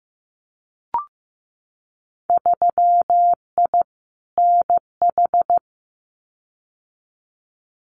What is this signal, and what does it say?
Morse code, heard as a keyed tone, about 15 wpm, 700 Hz. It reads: 3I NH
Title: Winter Field Day Exchanges Encore 15wpm